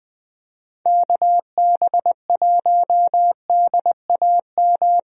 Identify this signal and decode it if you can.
Morse code, heard as a keyed tone, about 20 wpm, 700 Hz. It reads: KB1DAM
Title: USA Call Signs - 6 Characters - 20wpm